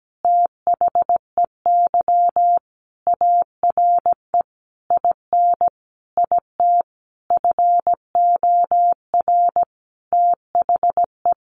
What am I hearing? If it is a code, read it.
Morse code, heard as a keyed tone, about 17 wpm, 700 Hz. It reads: THEY ARE IN IT FOR THE
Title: Sentences from Top 1300 Words 17wpm